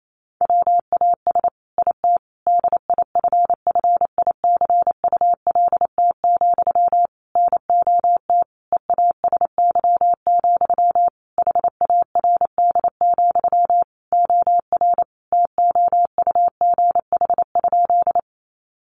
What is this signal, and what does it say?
Morse code, heard as a keyed tone, about 28 wpm, 700 Hz. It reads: WAS IT DIFFICULT, NOT EASY, HARD, OR TOUGH?
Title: Sentences from Top 1200 Words 28wpm